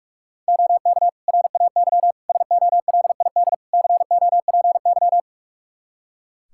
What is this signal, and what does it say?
Morse code, heard as a keyed tone, about 42 wpm, 700 Hz. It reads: OK RAY SOLID COPY
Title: Practice QSO - Exchange 3 - More Info - 45wpm